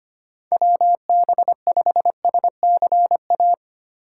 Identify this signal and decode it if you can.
Morse code, heard as a keyed tone, about 25 wpm, 700 Hz. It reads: WB5SCA
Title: USA Call Signs - 6 Characters - 4x Character Spacing - 25wpm